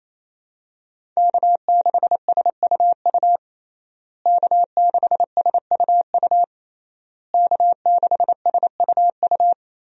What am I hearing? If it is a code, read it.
Morse code, heard as a keyed tone, about 27 wpm, 700 Hz. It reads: K6SUU K6SUU K6SUU
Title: USA Call Signs - 5 Characters - Repeated 3x - 28wpm